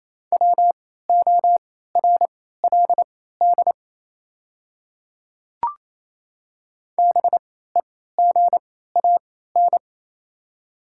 Morse code, heard as a keyed tone, about 28 wpm, spaced as for 16 wpm, 700 Hz. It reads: WORLD BEGAN
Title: 3x Character Spacing - Top 300 Words - 28wpm